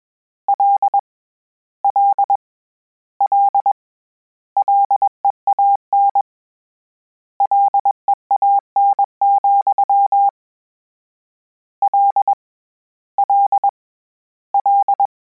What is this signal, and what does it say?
Morse code, heard as a keyed tone, about 21 wpm, 800 Hz. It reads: L L L LEAN LEAN, L L L
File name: Combinations / Combinations-mp3s / Combinations@20